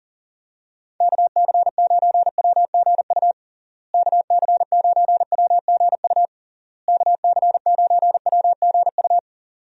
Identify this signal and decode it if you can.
Morse code, heard as a keyed tone, about 40 wpm, 700 Hz. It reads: KC9WGU KC9WGU KC9WGU
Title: USA Call Signs - 6 Characters - Repeated 3x - 40wpm